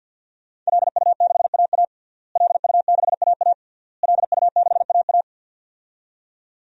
Morse code, heard as a keyed tone, about 47 wpm, 700 Hz. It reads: LU6AA LU6AA LU6AA
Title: DXCC-100 Call Signs - 5 Characters - Repeated 3x - 50wpm